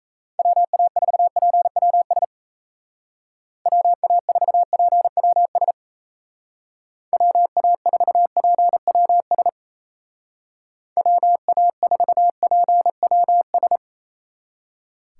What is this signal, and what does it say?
Morse code, heard as a keyed tone, about 33 wpm, 700 Hz. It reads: WA4PWS WA4PWS WA4PWS WA4PWS
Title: Speed Racing - USA 2x3 Call Signs 28wpm